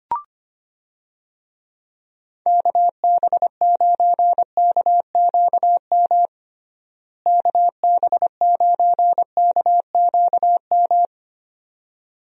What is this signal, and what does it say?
Morse code, heard as a keyed tone, about 25 wpm, 700 Hz. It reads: KB9KQM KB9KQM
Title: USA Call Signs - 6 Characters - Repeated 2x - 25wpm